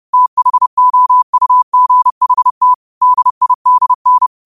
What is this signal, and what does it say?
Morse code, heard as a keyed tone, about 30 wpm, 1000 Hz. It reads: THOUGHT DIDN
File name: qotdfiles / 30wpm